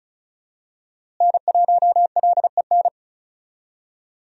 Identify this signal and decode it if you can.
Morse code, heard as a keyed tone, about 34 wpm, 700 Hz. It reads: N1LEN
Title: USA Call Signs - 5 Characters - Repeated 3x - 35wpm